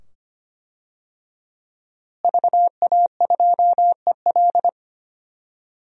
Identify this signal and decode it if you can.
Morse code, heard as a keyed tone, about 25 wpm, 700 Hz. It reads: VA2EL
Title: DXCC-100 Call Signs - 5 Characters - Repeated 2x - 25wpm